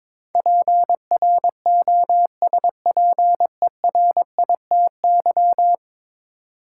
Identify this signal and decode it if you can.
Morse code, heard as a keyed tone, about 21 wpm, 700 Hz. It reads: PROSPERITY
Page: morse-code-ninja-practice-22wpm